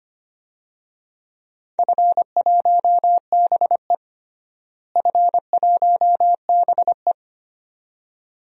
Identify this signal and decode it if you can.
Morse code, heard as a keyed tone, about 25 wpm, 700 Hz. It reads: F1BE F1BE
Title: DXCC-100 Call Signs - 4 Characters - Repeated 2x - 25wpm